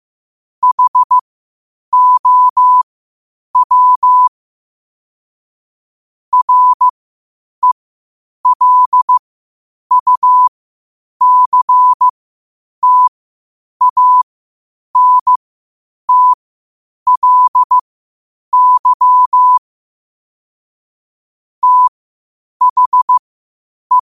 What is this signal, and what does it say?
Morse code, heard as a keyed tone, about 15 wpm, spaced as for 8 wpm, 1000 Hz. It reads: HOW RELUCTANTLY THE
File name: qotdfiles / 8wpm